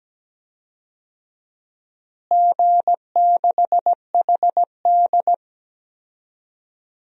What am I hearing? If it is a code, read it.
Morse code, heard as a keyed tone, about 17 wpm, 700 Hz. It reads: G6HD
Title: DXCC-100 Call Signs - 4 Characters - Repeated 2x - 17wpm